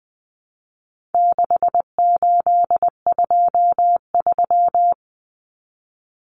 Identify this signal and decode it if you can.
Morse code, heard as a keyed tone, about 20 wpm, 700 Hz. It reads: 6823
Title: Numbers - 4-Digit 20wpm